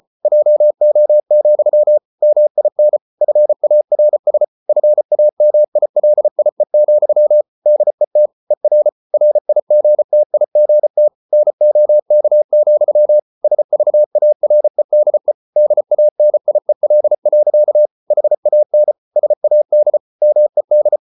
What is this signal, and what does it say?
Morse code, heard as a keyed tone, about 34 wpm, 600 Hz. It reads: JO, MIN FARS FAMILIE, DET ER RIGTIGT NOK, SVAREDE DANIEL. HAN SAD MED
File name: Ferie QRQ 34wpm